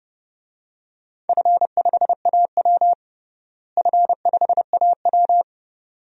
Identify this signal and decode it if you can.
Morse code, heard as a keyed tone, about 30 wpm, 700 Hz. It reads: F5AW F5AW
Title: DXCC-100 Call Signs - 4 Characters - Repeated 2x - 30wpm